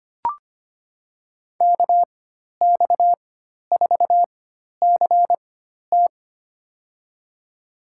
Morse code, heard as a keyed tone, about 25 wpm, spaced as for 12 wpm, 700 Hz. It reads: KX4CT
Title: USA Call Signs - 5 Characters - 4x Character Spacing - 25wpm